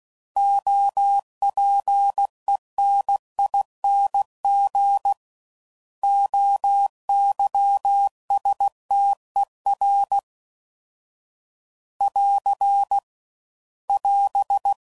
Morse code, heard as a keyed tone, about 16 wpm, 800 Hz. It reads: OPENING OYSTER <AR> <AS>
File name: Prefixes / By First Letter / Os